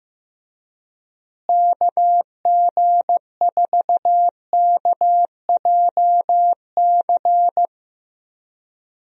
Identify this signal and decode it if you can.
Morse code, heard as a keyed tone, about 15 wpm, 700 Hz. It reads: KG4KJC
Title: USA Call Signs - 6 Characters - Repeated 2x - 15wpm